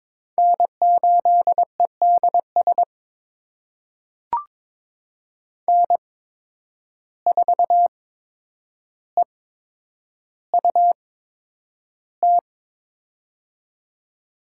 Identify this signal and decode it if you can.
Morse code, heard as a keyed tone, about 22 wpm, spaced as for 9 wpm, 700 Hz. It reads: N8EDS N4EUT
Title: USA Call Signs - 5 Characters - 8x Character Spacing - 22wpm